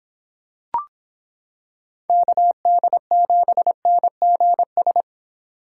Morse code, heard as a keyed tone, about 25 wpm, 700 Hz. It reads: KD7NGS